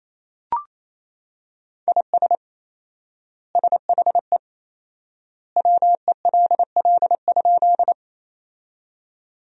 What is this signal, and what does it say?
Morse code, heard as a keyed tone, about 28 wpm, 700 Hz. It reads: IS SHE WELL?
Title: 4x Word-Spacing - Sentences from Top 200 Words - 28wpm